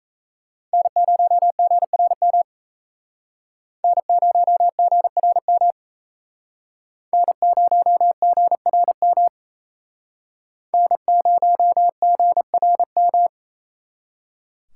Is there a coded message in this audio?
Yes: Morse code from 0.7 to 13.3 s, heard as a keyed tone, about 36 wpm, 700 Hz. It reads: N0GRM N0GRM N0GRM N0GRM